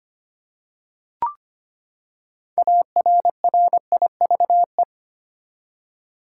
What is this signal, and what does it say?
Morse code, heard as a keyed tone, about 25 wpm, 700 Hz. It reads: ARRIVE
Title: Six Letter Words - 25wpm